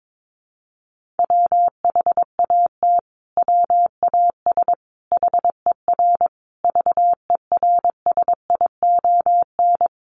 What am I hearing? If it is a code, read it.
Morse code, heard as a keyed tone, about 22 wpm, 700 Hz. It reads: WHAT WAS HER VERSION
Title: Sentences from Top 1400 Words 22wpm